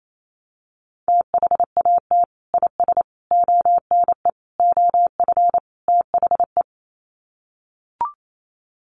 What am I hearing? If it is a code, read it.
Morse code, heard as a keyed tone, about 28 wpm, 700 Hz. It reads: THAT IS ONE OF THE
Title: Sets of 5 Words Encore 28wpm